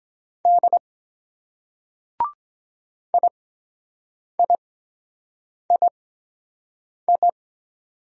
Morse code, heard as a keyed tone, about 23 wpm, spaced as for 7 wpm, 700 Hz. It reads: D IIII